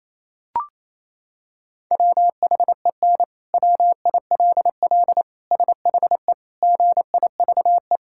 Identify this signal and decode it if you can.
Morse code, heard as a keyed tone, about 28 wpm, 700 Hz. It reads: WHEN WILL SHE GIVE
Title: Sentences from Top 1000 Words - Review Part 5 - 28wpm